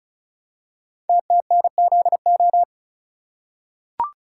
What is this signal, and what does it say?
Morse code, heard as a keyed tone, about 36 wpm, 700 Hz. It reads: TTNZO
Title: Code Groups - 5 Letters - 35wpm